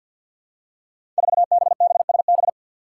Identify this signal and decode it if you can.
Morse code, heard as a keyed tone, about 52 wpm, 700 Hz. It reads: 4BBSB
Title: Five Challenging Letter-Number Combinations 50wpm